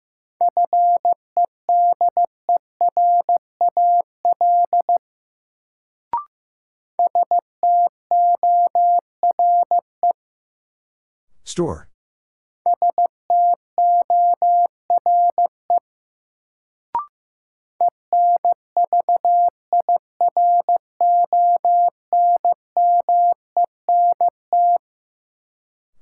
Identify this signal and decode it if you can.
Morse code, heard as a keyed tone, about 15 wpm, 700 Hz. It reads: FEDERAL STORE STORE ENVIRONMENT
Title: Sentences from Top 1000 Words - Review Part 5 - New Word Drill - 15wpm